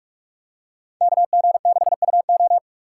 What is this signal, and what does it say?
Morse code, heard as a keyed tone, about 43 wpm, 700 Hz. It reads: KG6UO
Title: USA Call Signs - 5 Characters - 6x Character Spacing - 45wpm